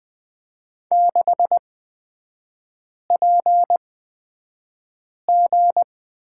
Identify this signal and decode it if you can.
Morse code, heard as a keyed tone, about 20 wpm, spaced as for 5 wpm, 690 Hz. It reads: 6PG
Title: 229 Lesson 37e - Letter x - Three Characters 20wpm